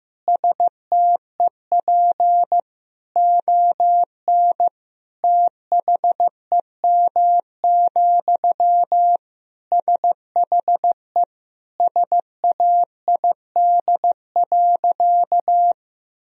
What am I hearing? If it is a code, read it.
Morse code, heard as a keyed tone, about 15 wpm, 700 Hz. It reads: STEP ON THEM, SHE SAID.
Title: Sentences from Top 1000 Words - Review Part 2 - 15wpm